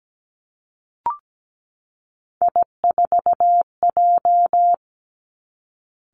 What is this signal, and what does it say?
Morse code, heard as a keyed tone, about 17 wpm, 700 Hz. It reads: I4J
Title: Three Challenging Letter-Number Combinations 17wpm